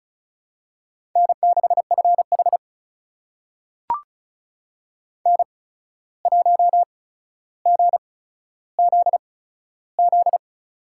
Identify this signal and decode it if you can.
Morse code, heard as a keyed tone, about 36 wpm, spaced as for 14 wpm, 700 Hz. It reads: N6FH N1GZZ
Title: USA Call Signs - 8x Character Spacing - 35wpm